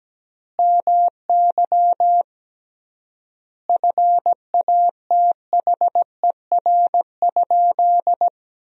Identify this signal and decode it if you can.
Morse code, heard as a keyed tone, about 17 wpm, 700 Hz. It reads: MY FATHER?
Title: 3x Word-Spacing - Sentences from Top 400 Words - 17wpm